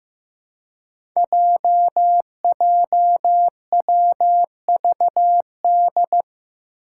Morse code, heard as a keyed tone, about 15 wpm, 700 Hz. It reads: JJWVD
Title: Code Groups - 5 Letters - 15wpm